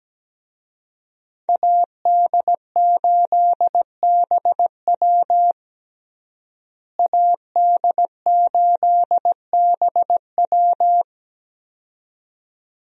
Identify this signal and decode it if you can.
Morse code, heard as a keyed tone, about 17 wpm, 700 Hz. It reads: AD8BW AD8BW
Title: USA Call Signs - 5 Characters - Repeated 2x - 17wpm